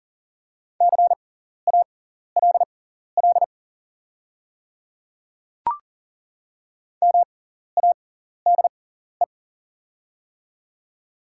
Morse code, heard as a keyed tone, about 37 wpm, spaced as for 14 wpm, 690 Hz. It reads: CALL MADE
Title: 6x Character Spacing - Top 300 Words - 40wpm